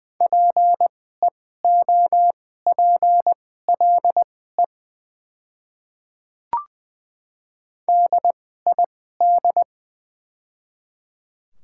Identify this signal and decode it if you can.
Morse code, heard as a keyed tone, about 20 wpm, spaced as for 15 wpm, 700 Hz. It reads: PEOPLE DID
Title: 2x Character Spacing - Top 100 Words - 20wpm